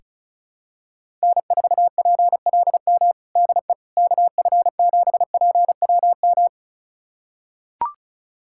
Morse code, heard as a keyed tone, about 36 wpm, 700 Hz. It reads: N4PLM DE KF7PWM
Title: USA Callsign DE Callsign 35wpm